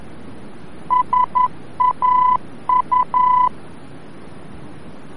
Morse code, heard as a keyed tone, about 11 wpm, 1000 Hz. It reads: SAU
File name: france / mx / SAU-VOR-F